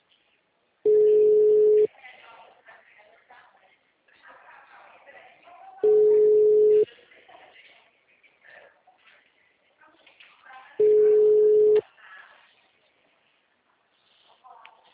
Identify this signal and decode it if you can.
Ringback tone (Europe, 425 Hz), with speech.
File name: out- -2006- - - .362